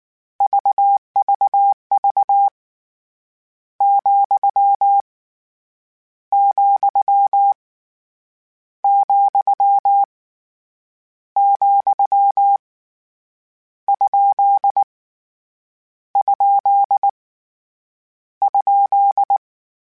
Morse code, heard as a keyed tone, about 19 wpm, 800 Hz. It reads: VVV , , , , ? ? ?
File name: Misc / Misc-mp3s / Punct @18